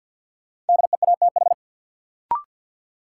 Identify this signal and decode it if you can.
Morse code, heard as a keyed tone, about 53 wpm, 700 Hz. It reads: DEATH
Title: Sentences from Top 1100 Words - New Word Drill - 50wpm